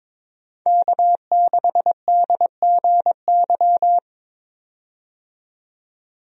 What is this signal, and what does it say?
Morse code, heard as a keyed tone, about 22 wpm, 700 Hz. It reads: K6DGY
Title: USA Call Signs - 5 Characters - 22wpm